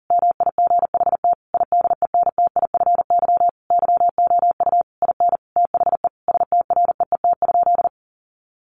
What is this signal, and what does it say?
Morse code, heard as a keyed tone, about 40 wpm, 700 Hz. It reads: MIGHT IDENTIFY YOU IN THE STREET?
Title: Sentences from Top 900 Words 40wpm